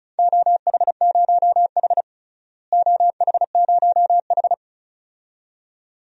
Morse code, heard as a keyed tone, about 34 wpm, 700 Hz. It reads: OH0H OH0H
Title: DXCC-100 Call Signs - 4 Characters - Repeated 2x - 35wpm